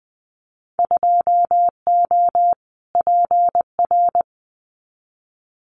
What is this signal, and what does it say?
Morse code, heard as a keyed tone, about 19 wpm, 700 Hz. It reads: 2O PR
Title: Winter Field Day Exchanges Encore 20wpm